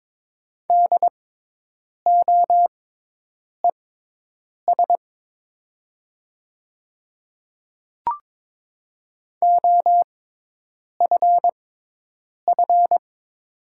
Morse code, heard as a keyed tone, about 22 wpm, spaced as for 8 wpm, 700 Hz. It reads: DOES OFF
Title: 6x Character Spacing - Top 200 Words - 22wpm